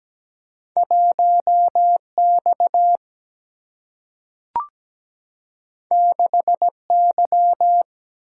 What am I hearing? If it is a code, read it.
Morse code, heard as a keyed tone, about 17 wpm, 700 Hz. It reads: 1X 6Y
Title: Single-Letter Single-Number 17wpm